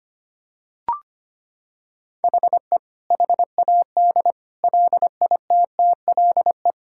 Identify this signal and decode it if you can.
Morse code, heard as a keyed tone, about 25 wpm, 700 Hz. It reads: HE HAD LITTLE
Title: Sentences from Top 1600 Words 25wpm